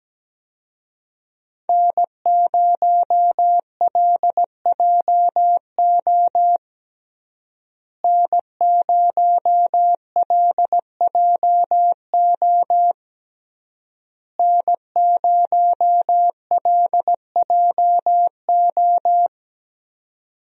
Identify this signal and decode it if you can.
Morse code, heard as a keyed tone, about 17 wpm, 700 Hz. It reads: N0LJO N0LJO N0LJO
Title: USA Call Signs - 5 Characters - Repeated 3x - 17wpm